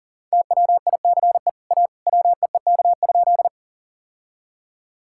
Morse code, heard as a keyed tone, about 37 wpm, 700 Hz. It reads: TWICE A WEEK?
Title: Sentences from Top 1600 Words 40wpm